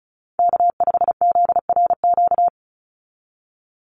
Morse code, heard as a keyed tone, about 34 wpm, 700 Hz. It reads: K5ZRQ
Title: USA Call Signs 35wpm